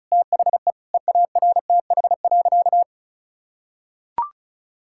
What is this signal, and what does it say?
Morse code, heard as a keyed tone, about 34 wpm, 700 Hz. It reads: THE EARTH.
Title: Sentences from Top 1000 Words - Review Part 6 - 35wpm